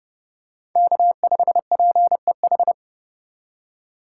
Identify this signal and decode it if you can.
Morse code, heard as a keyed tone, about 30 wpm, 700 Hz. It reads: K5PEH
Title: USA Call Signs - 2x Character Spacing - 30wpm